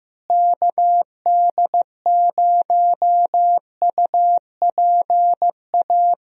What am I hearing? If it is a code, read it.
Morse code, heard as a keyed tone, about 15 wpm, 700 Hz. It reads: KD0UPA
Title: USA Call Signs - 6 Characters - Repeated 2x - 15wpm